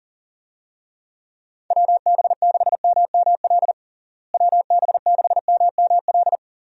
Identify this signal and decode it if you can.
Morse code, heard as a keyed tone, about 40 wpm, 700 Hz. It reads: WB6MML WB6MML
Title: USA Call Signs - 6 Characters - Repeated 3x - 40wpm